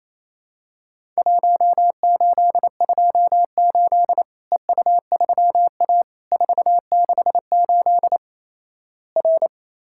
Morse code, heard as a keyed tone, about 28 wpm, 700 Hz. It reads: 1828 EU3A 468 R
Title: Worked All Europe DX Contest - QTC Exchange - 28wpm